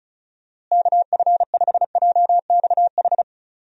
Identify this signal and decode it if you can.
Morse code, heard as a keyed tone, about 36 wpm, 700 Hz. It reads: KF5JXH
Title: USA Call Signs - 6 Characters - Repeated 2x - 35wpm